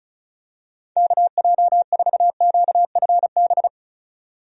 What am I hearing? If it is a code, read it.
Morse code, heard as a keyed tone, about 36 wpm, 700 Hz. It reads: KJ4QFB
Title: USA Call Signs - 6 Characters - 35wpm